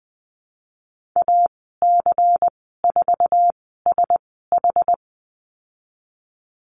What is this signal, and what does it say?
Morse code, heard as a keyed tone, about 20 wpm, spaced as for 15 wpm, 700 Hz. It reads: AC4SH
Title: 193 Lesson 31g - Letter v - Call Signs 20wpm